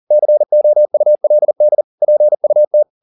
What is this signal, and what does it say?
Morse code, heard as a keyed tone, about 40 wpm, 600 Hz. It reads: COULD PUT